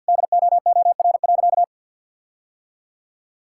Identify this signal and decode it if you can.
Morse code, heard as a keyed tone, about 52 wpm, 700 Hz. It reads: DOOR.
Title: Sentences from Top 1600 Words 50wpm